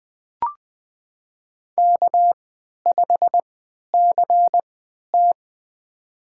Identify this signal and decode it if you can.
Morse code, heard as a keyed tone, about 20 wpm, spaced as for 11 wpm, 700 Hz. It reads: K5CT
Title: USA Call Signs - 3x Character Spacing - 20wpm